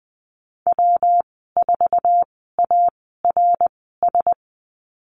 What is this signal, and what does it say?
Morse code, heard as a keyed tone, about 20 wpm, spaced as for 15 wpm, 700 Hz. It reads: W4ARS
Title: 167 Lesson 27g - Number 7 - Call Signs 20wpm